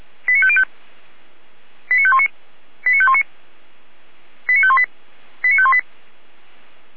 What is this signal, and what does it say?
Five-tone selective call (ZVEI), heard as quick runs of five tones, tones of about 70 ms. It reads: 89505 89519 89519 89518 89518
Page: /455/SDR/ZVEI/89485/2021.07.01/